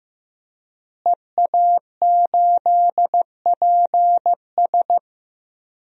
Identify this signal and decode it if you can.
Morse code, heard as a keyed tone, about 15 wpm, 700 Hz. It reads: EA8PS
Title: DXCC-100 Call Signs - 5 Characters - Repeated 3x - 15wpm